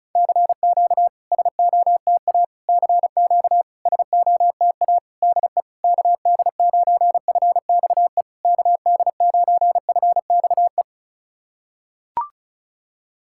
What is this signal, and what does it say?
Morse code, heard as a keyed tone, about 39 wpm, 700 Hz. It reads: CQ SOTA CQ SOTA DE KD9FXE KD9FXE
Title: SOTA QSO Practice 35wpm